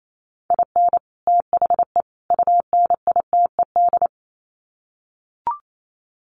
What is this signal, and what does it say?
Morse code, heard as a keyed tone, about 28 wpm, 700 Hz. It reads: IN THE UNITED
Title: Sets of 3 Words 28wpm